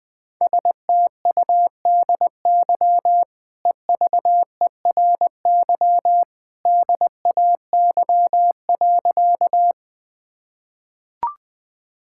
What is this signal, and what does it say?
Morse code, heard as a keyed tone, about 20 wpm, 700 Hz. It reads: STUDY EVERY DAY.
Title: Easy Sentences with Hints - 20wpm